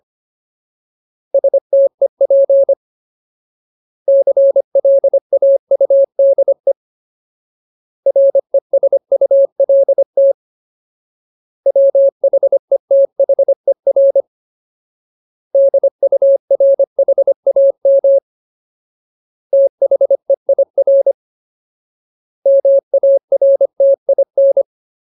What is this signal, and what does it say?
Morse code, heard as a keyed tone, about 25 wpm, 550 Hz. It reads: STEP CLAUDE RESULT WHETHER DURHAM THEIR MARTIN
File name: Ultimate Word List / 25WPM Ultimate